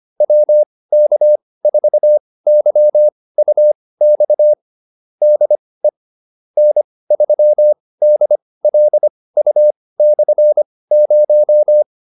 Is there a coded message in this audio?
Yes: Morse code from 0.2 to 11.8 s, heard as a keyed tone, about 25 wpm, spaced as for 18 wpm, 600 Hz. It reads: WK4YUX DE N3DLU/0